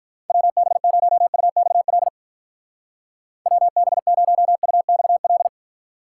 Morse code, heard as a keyed tone, about 45 wpm, 700 Hz. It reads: WB0UXL WB0UXL